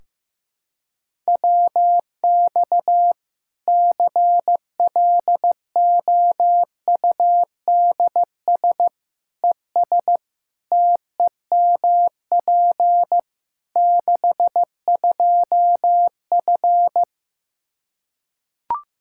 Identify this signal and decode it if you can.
Morse code, heard as a keyed tone, about 16 wpm, 700 Hz. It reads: WX CLOUDS ES TEMP 62F
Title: K1USN SST QSO Practice 15wpm